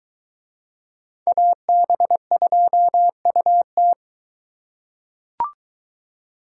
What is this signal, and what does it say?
Morse code, heard as a keyed tone, about 23 wpm, 700 Hz. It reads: AB2UT